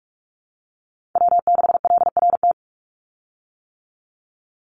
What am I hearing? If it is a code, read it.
Morse code, heard as a keyed tone, about 44 wpm, 700 Hz. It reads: W6LRT
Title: USA Call Signs 45wpm